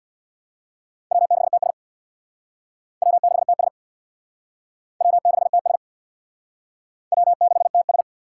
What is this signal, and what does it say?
Morse code, heard as a keyed tone, about 63 wpm, 700 Hz. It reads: W6TS W6TS W6TS W6TS